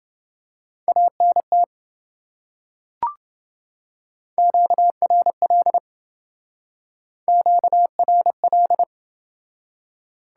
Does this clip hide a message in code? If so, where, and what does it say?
Yes: Morse code from 0.9 to 8.8 s, heard as a keyed tone, about 28 wpm, 700 Hz. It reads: ANT QRL QRL